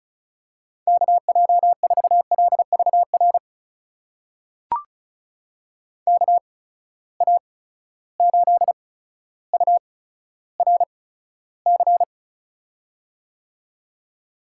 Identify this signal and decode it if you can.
Morse code, heard as a keyed tone, about 36 wpm, spaced as for 15 wpm, 700 Hz. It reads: KJ4LVR KA8URC
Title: USA Call Signs - 8x Character Spacing - 35wpm